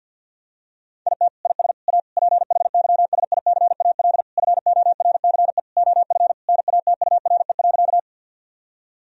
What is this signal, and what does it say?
Morse code, heard as a keyed tone, about 50 wpm, 700 Hz. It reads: IT IS A PHYSICAL FORCE OF NATURE.
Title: Sentences from Top 1000 Words - Review Part 4 - 50wpm